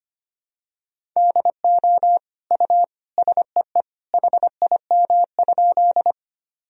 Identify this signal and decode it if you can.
Morse code, heard as a keyed tone, about 25 wpm, 700 Hz. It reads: DO U SEE HIM?
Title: Sentences from Top 100 Words and Abbreviations 25wpm